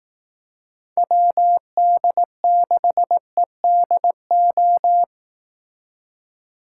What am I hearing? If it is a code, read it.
Morse code, heard as a keyed tone, about 18 wpm, 700 Hz. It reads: WD6EDO
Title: Speed Racing - USA 2x3 Call Signs 15wpm